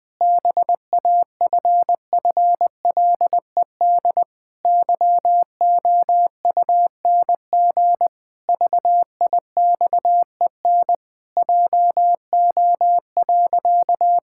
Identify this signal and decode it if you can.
Morse code, heard as a keyed tone, about 20 wpm, 700 Hz. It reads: BAFFLED YOUNG VIXEN JO.
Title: Pangram Challenge v2 - 20wpm